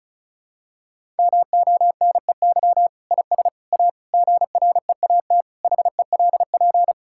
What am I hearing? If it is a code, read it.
Morse code, heard as a keyed tone, about 34 wpm, 700 Hz. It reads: MONEY IS A GREAT HELP
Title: Sentences from Top 1600 Words 35wpm